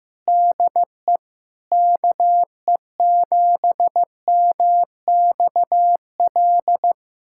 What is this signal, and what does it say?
Morse code, heard as a keyed tone, about 15 wpm, 700 Hz. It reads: DE KE7MXL
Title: USA Callsign DE Callsign 15wpm